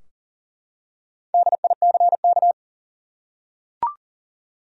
Morse code, heard as a keyed tone, about 39 wpm, 700 Hz. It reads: DICK
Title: USA Common Names - Max 5 Letters - 40wpm